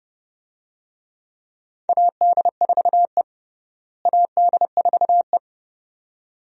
Morse code, heard as a keyed tone, about 30 wpm, 700 Hz. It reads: AD4E AD4E
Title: USA Call Signs - 4 Characters - Repeated 2x - 30wpm